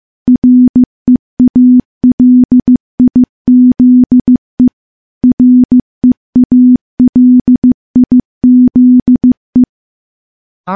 Morse code, heard as a keyed tone, about 15 wpm, 250 Hz. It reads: REALIZE REALIZE
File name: morse / morse-training-15WPM